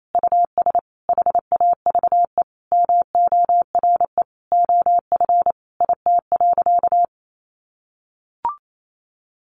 Morse code, heard as a keyed tone, about 28 wpm, 700 Hz. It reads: US HAVE MORE OF IT.
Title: Sentences from Top 1300 Words 28wpm